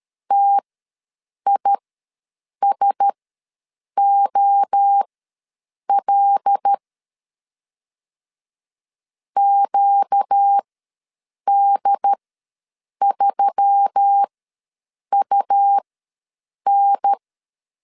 Morse code, heard as a keyed tone, about 13 wpm, spaced as for 7 wpm, 800 Hz. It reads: TISOL QD3UN
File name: LEZIONI / 5 LBVC / 5 L ws 12wpm 2958